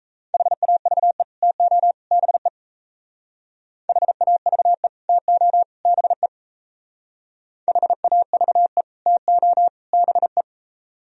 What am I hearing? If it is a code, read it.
Morse code, heard as a keyed tone, about 34 wpm, 700 Hz. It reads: HAVE TO BE HAVE TO BE HAVE TO BE
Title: Speed-Racing - Sets of 3 Words 28wpm